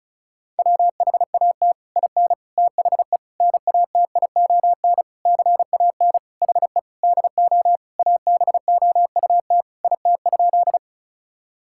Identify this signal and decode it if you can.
Morse code, heard as a keyed tone, about 36 wpm, 700 Hz. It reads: WHAT IN THE NATION CAN HE DO ABOUT IT?
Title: Sentences from Top 1000 Words - Review Part 6 - 35wpm